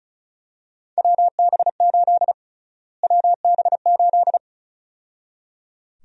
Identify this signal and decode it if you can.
Morse code, heard as a keyed tone, about 36 wpm, 700 Hz. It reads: WB8 WB8
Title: USA Call Sign Prefixes - Repeated 2x - 35wpm